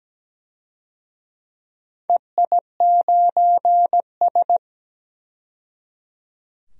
Morse code, heard as a keyed tone, about 17 wpm, 700 Hz. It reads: EI9S
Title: DXCC-100 Call Signs - 4 Characters - 17wpm